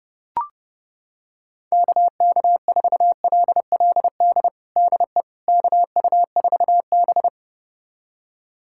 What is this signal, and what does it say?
Morse code, heard as a keyed tone, about 30 wpm, 700 Hz. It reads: KK4LLD DE KU4B
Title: USA Callsign DE Callsign 30wpm